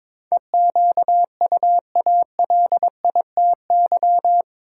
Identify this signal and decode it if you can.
Morse code, heard as a keyed tone, about 22 wpm, 700 Hz. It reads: EQUALITY
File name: practice / 22wpm / Eight Letter Words - 22wpm